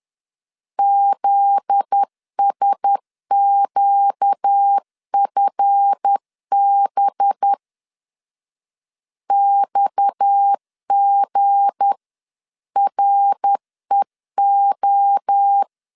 Morse code, heard as a keyed tone, about 10 wpm, 800 Hz. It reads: ZSQFB XGREO
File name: LEZIONI / 7 GF9Z / 7 GF9Z 10wpm 2611